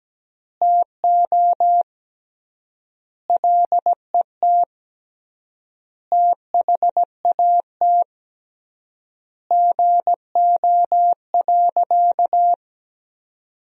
Morse code, heard as a keyed tone, about 17 wpm, 700 Hz. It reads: TO LET THAT GO.
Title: 3x Word-Spacing - Sentences from Top 300 Words - 17wpm